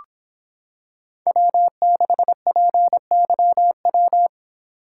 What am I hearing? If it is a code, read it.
Morse code, heard as a keyed tone, about 25 wpm, 700 Hz. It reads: W6PYW